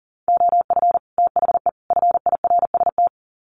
Morse code, heard as a keyed tone, about 40 wpm, 700 Hz. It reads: OF THE FIRST
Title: Sets of 3 Words 40wpm